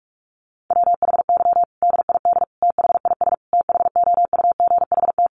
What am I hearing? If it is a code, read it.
Morse code, heard as a keyed tone, about 44 wpm, 700 Hz. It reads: WHY DID THIS THOUGHT
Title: Sentences from Top 900 Words 45wpm